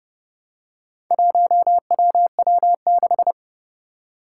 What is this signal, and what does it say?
Morse code, heard as a keyed tone, about 30 wpm, 700 Hz. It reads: 1WW6
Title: Four Challenging Letter-Number Combinations 30wpm